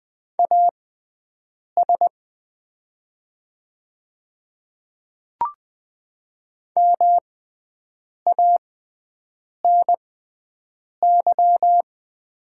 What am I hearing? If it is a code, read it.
Morse code, heard as a keyed tone, about 20 wpm, spaced as for 7 wpm, 700 Hz. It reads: AS MANY
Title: 6x Character Spacing - Top 500 Words - 20wpm